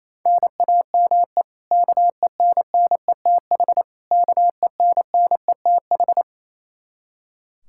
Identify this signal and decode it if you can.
Morse code, heard as a keyed tone, about 27 wpm, 700 Hz. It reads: NAME KENNETH KENNETH
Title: Practice QSO - Exchange 1 - Beginning - 28wpm